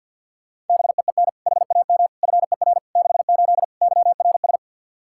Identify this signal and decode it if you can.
Morse code, heard as a keyed tone, about 52 wpm, 700 Hz. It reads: BEEN HAM FER 68 YRS
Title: Practice QSO - Exchange 3 - More Info - 50wpm